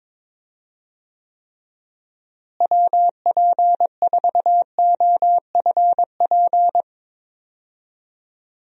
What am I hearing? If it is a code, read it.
Morse code, heard as a keyed tone, about 22 wpm, 700 Hz. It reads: WP4OFP
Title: USA Call Signs - 6 Characters - Repeated 2x - 22wpm